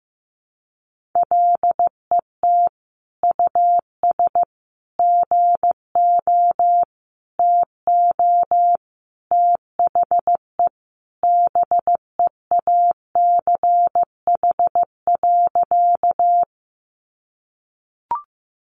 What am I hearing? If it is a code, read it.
Morse code, heard as a keyed tone, about 15 wpm, 700 Hz. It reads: LET US GO TO THE BEACH.
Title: Sentences from Top 1500 Words 15wpm